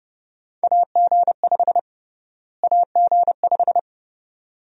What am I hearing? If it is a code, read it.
Morse code, heard as a keyed tone, about 29 wpm, 700 Hz. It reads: AG5 AG5
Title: USA Call Sign Prefixes - Repeated 2x - 30wpm